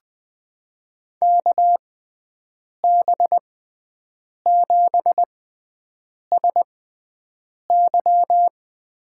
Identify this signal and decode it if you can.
Morse code, heard as a keyed tone, about 21 wpm, spaced as for 7 wpm, 700 Hz. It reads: KB7SY
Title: USA Call Signs - 6 Characters - 6x Character Spacing - 20wpm